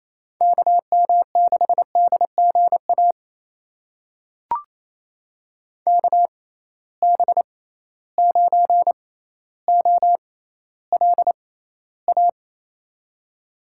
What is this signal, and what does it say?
Morse code, heard as a keyed tone, about 27 wpm, spaced as for 14 wpm, 700 Hz. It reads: KM6DGA KB9OLA
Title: USA Call Signs - 6 Characters - 6x Character Spacing - 28wpm